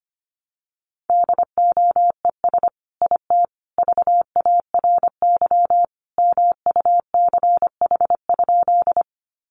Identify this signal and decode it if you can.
Morse code, heard as a keyed tone, about 25 wpm, 700 Hz. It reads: DOES IT VARY MUCH?
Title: Sentences from Top 1200 Words 25wpm